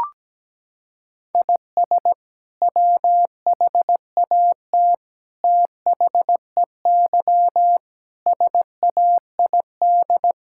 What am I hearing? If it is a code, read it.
Morse code, heard as a keyed tone, about 17 wpm, 700 Hz. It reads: IS WHAT THEY SAID
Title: Sentences from Top 1000 Words - Review Part 5 - 17wpm